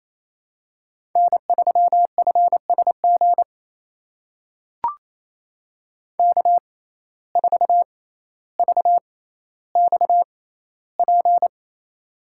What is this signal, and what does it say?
Morse code, heard as a keyed tone, about 28 wpm, spaced as for 14 wpm, 700 Hz. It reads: N3FSG K4VXP
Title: USA Call Signs - 5 Characters - 6x Character Spacing - 28wpm